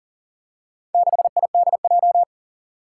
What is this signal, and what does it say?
Morse code, heard as a keyed tone, about 40 wpm, 700 Hz. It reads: 6IDJ
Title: Four Challenging Letter-Number Combinations 40wpm